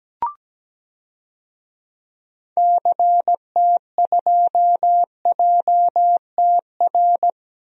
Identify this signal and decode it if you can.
Morse code, heard as a keyed tone, about 17 wpm, 700 Hz. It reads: CT2JTR
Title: DXCC-100 Call Signs - 6 Characters - Repeated 3x - 17wpm